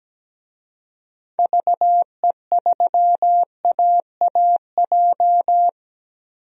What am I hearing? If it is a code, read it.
Morse code, heard as a keyed tone, about 17 wpm, 700 Hz. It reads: VE3AAJ
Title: DXCC-100 Call Signs - 6 Characters - Repeated 3x - 17wpm